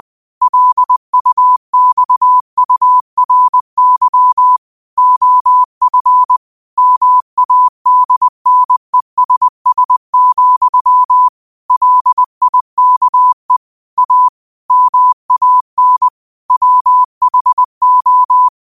Morse code, heard as a keyed tone, about 20 wpm, 1000 Hz. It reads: LUXURY OF MADNESS, LIKE A MAN WHO
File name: qotdfiles / 20wpm